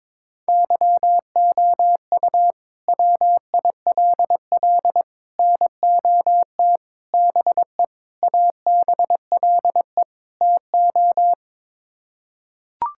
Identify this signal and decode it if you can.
Morse code, heard as a keyed tone, about 22 wpm, 700 Hz. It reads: YOU WILL NOT BE ABLE TO
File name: practice / 22wpm / Sets of 6 Words - 22wpm